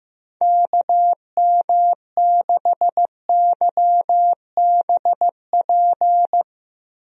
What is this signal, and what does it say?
Morse code, heard as a keyed tone, about 15 wpm, 700 Hz. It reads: KM6YBP
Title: USA Call Signs - 6 Characters - Repeated 3x - 15wpm